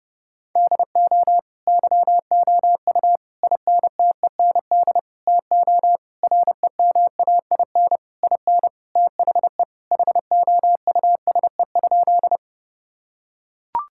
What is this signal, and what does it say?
Morse code, heard as a keyed tone, about 32 wpm, 700 Hz. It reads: DO YOU INTEND TO REMAIN IN THE HOUSE?
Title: Sentences from Top 1600 Words 30wpm